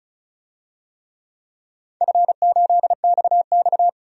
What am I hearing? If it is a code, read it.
Morse code, heard as a keyed tone, about 36 wpm, 700 Hz. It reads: F8XX
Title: DXCC-100 Call Signs - 4 Characters - 35wpm